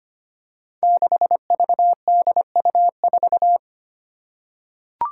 Morse code, heard as a keyed tone, about 25 wpm, 700 Hz. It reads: 6VDU4
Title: Five Challenging Letter-Number Combinations 25wpm